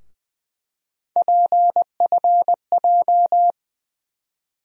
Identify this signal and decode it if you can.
Morse code, heard as a keyed tone, about 20 wpm, 700 Hz. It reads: PFJ
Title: USA Call Sign Postfixes - 3 Letters 20wpm